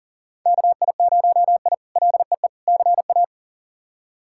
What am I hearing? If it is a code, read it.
Morse code, heard as a keyed tone, about 40 wpm, 700 Hz. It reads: KI0I LEE CA
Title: K1USN SST QSO Practice - 40wpm